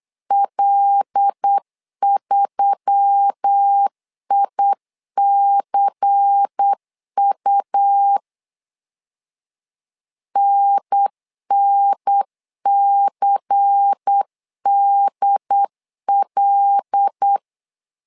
Morse code, heard as a keyed tone, about 8 wpm, 800 Hz. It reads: L3ICU NNCDL
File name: LEZIONI / 5 LBVC / 5 LBVC 8wpm 2935